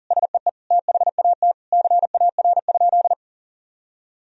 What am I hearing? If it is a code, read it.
Morse code, heard as a keyed tone, about 37 wpm, 700 Hz. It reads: SEE THAT CAR?
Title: Sentences from Top 1000 Words - Review Part 1 - 40wpm